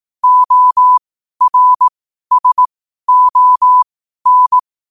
Morse code, heard as a keyed tone, about 18 wpm, spaced as for 12 wpm, 1000 Hz. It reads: ORSON
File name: qotdfiles / 12wpm